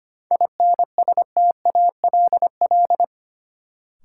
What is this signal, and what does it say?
Morse code, heard as a keyed tone, about 25 wpm, 700 Hz. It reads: INSTALL
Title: Sentences from Top 1600 Words - New Word Drill Alt - 25wpm